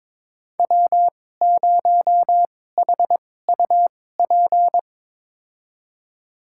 Morse code, heard as a keyed tone, about 22 wpm, spaced as for 16 wpm, 700 Hz. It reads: W0HUP
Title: USA Call Signs - 5 Characters - 2x Character Spacing - 22wpm